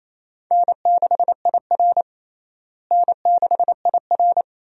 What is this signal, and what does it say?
Morse code, heard as a keyed tone, about 28 wpm, 700 Hz. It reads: N6IR N6IR
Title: USA Call Signs - 4 Characters - Repeated 2x - 28wpm